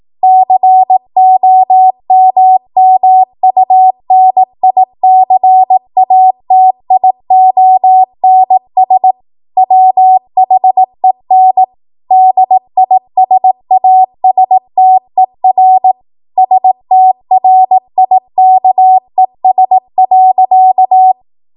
Morse code, heard as a keyed tone, about 18 wpm, 750 Hz. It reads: COMMUNICATIONS WHEN DISASTER STRIKES.
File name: Morse / 18WPM